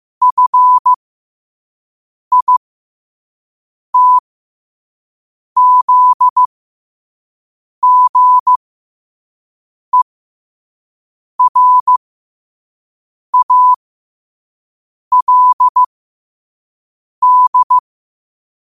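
Morse code, heard as a keyed tone, about 15 wpm, spaced as for 5 wpm, 1000 Hz. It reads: FITZGERALD
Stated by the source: Morse code sent at 5 wpm.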